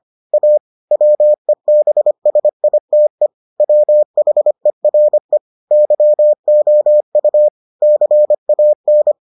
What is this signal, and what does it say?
Morse code, heard as a keyed tone, about 25 wpm, 600 Hz. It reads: A WEBSITE WHERE YOU CAN